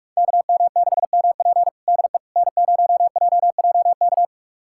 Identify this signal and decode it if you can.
Morse code, heard as a keyed tone, about 45 wpm, 700 Hz. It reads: KM6MP DE N0JJK
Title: USA Callsign DE Callsign - 5 Characters - 45wpm